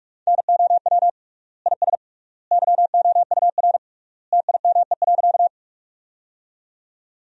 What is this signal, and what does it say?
Morse code, heard as a keyed tone, about 46 wpm, 700 Hz. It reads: NOW IS YOUR TIME.
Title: 3x Word-Spacing - Sentences from Top 100 Words - 45wpm